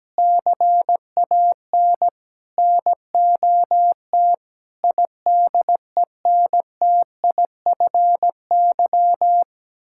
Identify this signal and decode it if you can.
Morse code, heard as a keyed tone, about 17 wpm, 700 Hz. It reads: CAN NOT IDENTIFY
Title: Sentences from Top 1000 Words - Review Part 6 - 17wpm